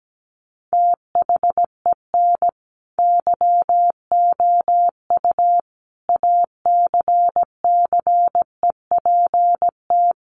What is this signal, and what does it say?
Morse code, heard as a keyed tone, about 17 wpm, 700 Hz. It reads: THEN YOU ACCEPT
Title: Sentences from Top 1300 Words 17wpm